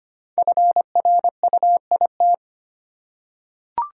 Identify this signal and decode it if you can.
Morse code, heard as a keyed tone, about 25 wpm, 700 Hz. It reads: FRUIT
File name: 5 Letter Words - First 3 Letters Given - 25wpm